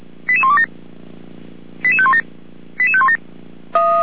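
Five-tone selective call (ZVEI), heard as quick runs of five tones, tones of about 70 ms. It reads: 88127 88517 88517
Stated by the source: ZVEI five-tone selective call received on an SDR.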